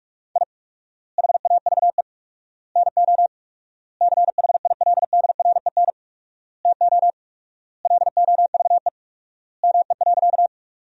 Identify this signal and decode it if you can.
Morse code, heard as a keyed tone, about 46 wpm, 700 Hz. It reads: I HAVE NO CHILDREN TO LOVE ME.
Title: 4x Word-Spacing - Sentences from Top 400 Words - 45wpm